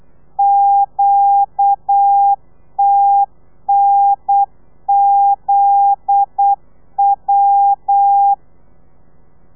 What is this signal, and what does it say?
Morse code, heard as a keyed tone, about 8 wpm, 800 Hz. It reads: QTNZW
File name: telegrafia / i2tfj / Lez9 35